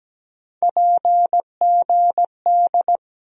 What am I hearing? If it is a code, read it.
Morse code, heard as a keyed tone, about 17 wpm, 700 Hz. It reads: PGD
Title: USA Call Sign Postfixes - 3 Letters 17wpm